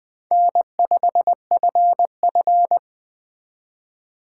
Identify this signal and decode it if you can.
Morse code, heard as a keyed tone, about 20 wpm, 700 Hz. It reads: N5FF
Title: Speed Racing - USA 1x2 Call Signs 17wpm